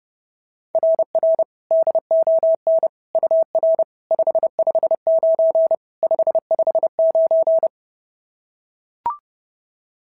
Morse code, heard as a keyed tone, about 26 wpm, 650 Hz. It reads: RR DON UR 559 559
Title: SOTA QSO Practice 30wpm